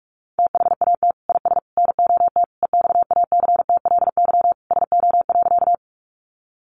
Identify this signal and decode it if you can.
Morse code, heard as a keyed tone, about 44 wpm, 700 Hz. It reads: THAT IS NOT EXACTLY SO.
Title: Sentences from Top 900 Words 45wpm